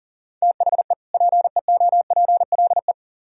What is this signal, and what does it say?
Morse code, heard as a keyed tone, about 52 wpm, 700 Hz. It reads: THE PEOPLE